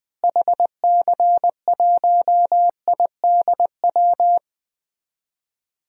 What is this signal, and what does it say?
Morse code, heard as a keyed tone, about 20 wpm, 700 Hz. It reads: HC1IDW
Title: DXCC-100 Call Signs - 6 Characters - Repeated 3x - 20wpm